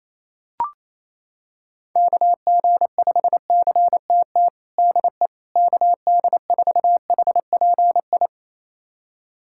Morse code, heard as a keyed tone, about 28 wpm, 700 Hz. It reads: KG5CTT DE KD4HPI
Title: USA Callsign DE Callsign 28wpm